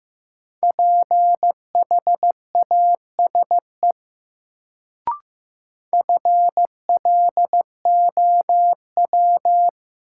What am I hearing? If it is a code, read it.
Morse code, heard as a keyed tone, about 15 wpm, 700 Hz. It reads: PHASE FLOW
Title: Sentences from Top 1200 Words - New Word Drill - 15wpm